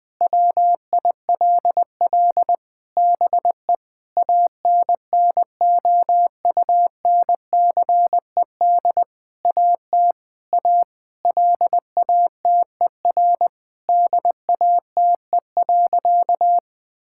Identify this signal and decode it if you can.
Morse code, heard as a keyed tone, about 21 wpm, 700 Hz. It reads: WILL BE ANNOUNCED AT A LATER DATE.
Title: Sentences with 7-Word Phrases - 20wpm